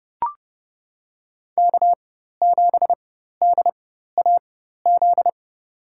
Morse code, heard as a keyed tone, about 32 wpm, spaced as for 14 wpm, 700 Hz. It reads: K7DAZ
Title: USA Call Signs - 5 Characters - 4x Character Spacing - 30wpm